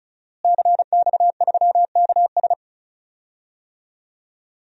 Morse code, heard as a keyed tone, about 36 wpm, 700 Hz. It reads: CX3KS
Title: DXCC-100 Call Signs - 5 Characters - Repeated 2x - 35wpm